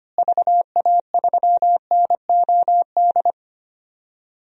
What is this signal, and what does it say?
Morse code, heard as a keyed tone, about 25 wpm, 700 Hz. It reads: VA3NOD
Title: DXCC-100 Call Signs - 6 Characters - Repeated 2x - 25wpm